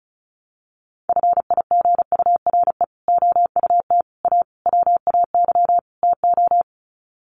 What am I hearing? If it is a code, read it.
Morse code, heard as a keyed tone, about 34 wpm, 700 Hz. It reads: FIGURE OUT A WAY TO
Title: Sets of 5 Words Encore2 35wpm